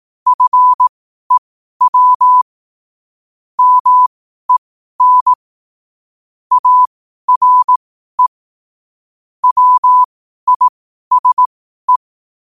Morse code, heard as a keyed tone, about 18 wpm, spaced as for 12 wpm, 1000 Hz. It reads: FEW MEN ARE WISE
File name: qotdfiles / 12wpm